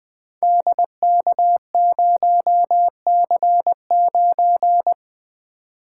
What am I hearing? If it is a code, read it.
Morse code, heard as a keyed tone, about 20 wpm, 700 Hz. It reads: DK0C9
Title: Code Groups - 5 Letter:Numbers - 20wpm